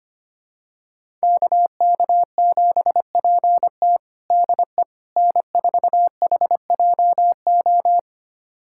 Morse code, heard as a keyed tone, about 25 wpm, 700 Hz. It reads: KK7PT DE N4HJO
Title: USA Callsign DE Callsign - 5 Characters - 25wpm